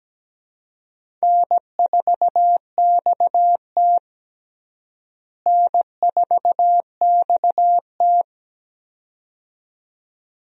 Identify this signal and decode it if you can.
Morse code, heard as a keyed tone, about 17 wpm, 700 Hz. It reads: N4XT N4XT
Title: USA Call Signs - 4 Characters - Repeated 2x - 17wpm